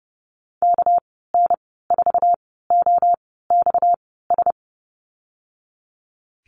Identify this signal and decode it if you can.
Morse code, heard as a keyed tone, about 30 wpm, spaced as for 17 wpm, 700 Hz. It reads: KN4OXS
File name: Lesson 37g - Letter x - Call Signs 30wpm